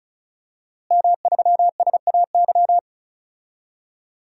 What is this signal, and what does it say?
Morse code, heard as a keyed tone, about 36 wpm, 700 Hz. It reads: M3SAY
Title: DXCC-100 Call Signs - 5 Characters - Repeated 3x - 35wpm